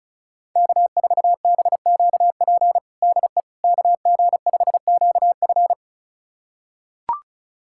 Morse code, heard as a keyed tone, about 34 wpm, 700 Hz. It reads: K4BQP DE KG5QF
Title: USA Callsign DE Callsign 35wpm